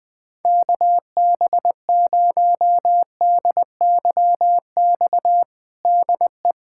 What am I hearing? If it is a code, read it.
Morse code, heard as a keyed tone, about 20 wpm, 700 Hz. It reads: KB0DYX DE
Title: USA Callsign DE Callsign - 6 Characters - 20wpm